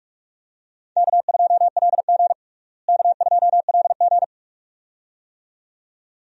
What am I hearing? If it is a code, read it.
Morse code, heard as a keyed tone, about 47 wpm, 700 Hz. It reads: K2LG K2LG
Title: USA Call Signs - 4 Characters - Repeated 2x - 45wpm